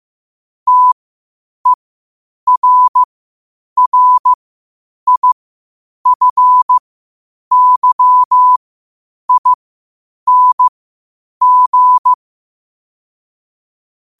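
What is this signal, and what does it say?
Morse code, heard as a keyed tone, about 15 wpm, spaced as for 8 wpm, 1000 Hz. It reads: TERRIFYING
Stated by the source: Morse code at 8 wpm.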